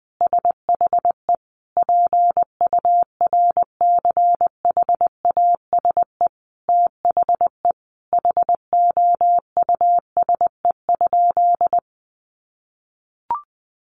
Morse code, heard as a keyed tone, about 20 wpm, 700 Hz. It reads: SHE PURCHASE THE HOUSE?
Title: Sentences from Top 800 Words 20wpm